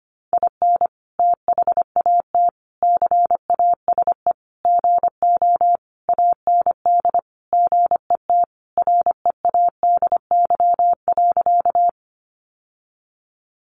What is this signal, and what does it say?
Morse code, heard as a keyed tone, about 25 wpm, 700 Hz. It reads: IN THAT CASE GO AND GET READY.
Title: Sentences from Top 700 Words 25wpm